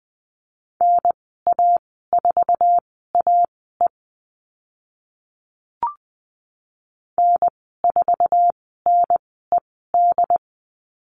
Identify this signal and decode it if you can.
Morse code, heard as a keyed tone, about 20 wpm, spaced as for 15 wpm, 700 Hz. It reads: NA4AE N4NED
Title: 069 Lesson 12g - Letter d - Call Signs 20wpm